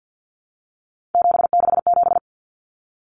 Morse code, heard as a keyed tone, about 50 wpm, 700 Hz. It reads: 767
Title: Common Cut Numbers - 3-Digit 50wpm